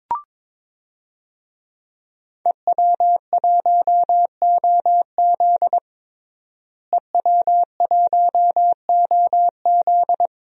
Morse code, heard as a keyed tone, about 22 wpm, 700 Hz. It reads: EW1OZ EW1OZ
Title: DXCC-100 Call Signs - 5 Characters - Repeated 3x - 22wpm